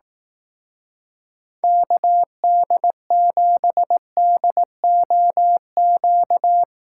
Morse code, heard as a keyed tone, about 18 wpm, 700 Hz. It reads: KD7DOQ
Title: Speed Racing - USA 2x3 Call Signs 15wpm